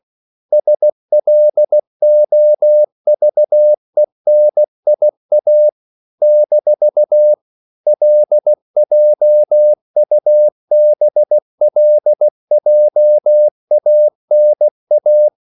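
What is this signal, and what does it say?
Morse code, heard as a keyed tone, about 16 wpm, 600 Hz. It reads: SLOVENIA - LJUBLJANA
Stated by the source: CW practice of national capitals at 16 words per minute, from ON6ZQ.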